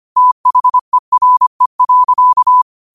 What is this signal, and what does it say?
Morse code, heard as a keyed tone, about 25 wpm, 1000 Hz. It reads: THERE.